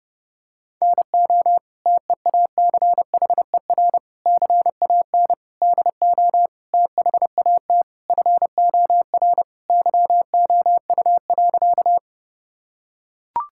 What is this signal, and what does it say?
Morse code, heard as a keyed tone, about 30 wpm, 700 Hz. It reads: NO TEACHER CAN DO THAT FOR YOU.
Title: Sentences from Top 1000 Words - Review Part 6 - 30wpm